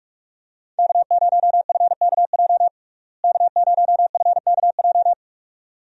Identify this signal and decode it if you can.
Morse code, heard as a keyed tone, about 45 wpm, 700 Hz. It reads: K0FKJ K0FKJ
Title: USA Call Signs - 5 Characters - Repeated 2x - 45wpm